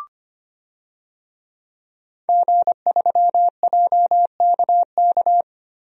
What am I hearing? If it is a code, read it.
Morse code, heard as a keyed tone, about 25 wpm, 700 Hz. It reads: G3JKK
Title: DXCC-100 Call Signs - 5 Characters - 25wpm